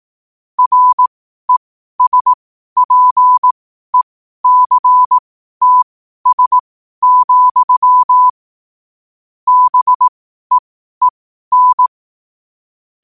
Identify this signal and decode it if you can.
Morse code, heard as a keyed tone, about 18 wpm, spaced as for 12 wpm, 1000 Hz. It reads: RESPECTS, BEEN